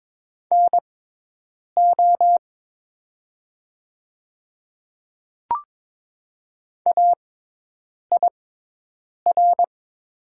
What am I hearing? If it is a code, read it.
Morse code, heard as a keyed tone, about 22 wpm, spaced as for 8 wpm, 700 Hz. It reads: NO AIR
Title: 6x Character Spacing - Top 200 Words - 22wpm